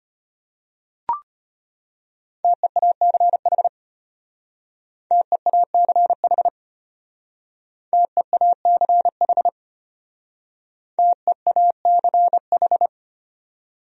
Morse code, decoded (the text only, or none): TEACH TEACH TEACH TEACH